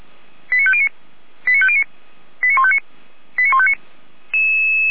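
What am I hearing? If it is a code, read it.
Five-tone selective call (ZVEI), heard as quick runs of five tones, tones of about 70 ms. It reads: 89509 89509 89159 89159
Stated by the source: ZVEI five-tone call 89160 received on SDR, from the feuerwehr-orlamuende.org (fire brigade Orlamünde) server.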